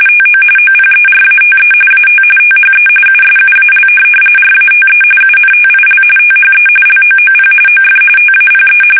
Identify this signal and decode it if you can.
RTTY, about 45.45 baud. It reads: TO OPERATE AT THE USUAL RATE UNTIL THE LOOP OF TAPE
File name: fsk850